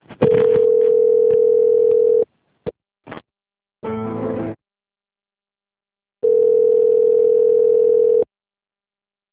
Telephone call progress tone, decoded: ringback tone

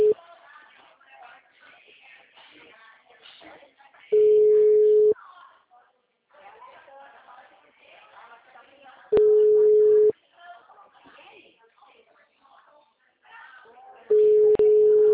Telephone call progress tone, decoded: ringback tone

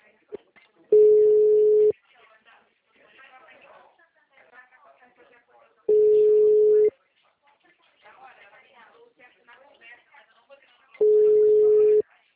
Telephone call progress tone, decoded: ringback tone

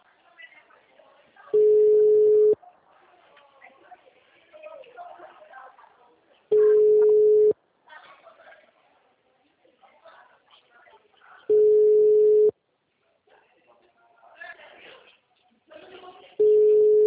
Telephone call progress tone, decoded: ringback tone

